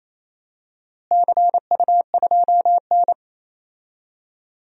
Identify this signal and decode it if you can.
Morse code, heard as a keyed tone, about 27 wpm, 700 Hz. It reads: CU2N
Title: DXCC-100 Call Signs - 4 Characters - Repeated 2x - 28wpm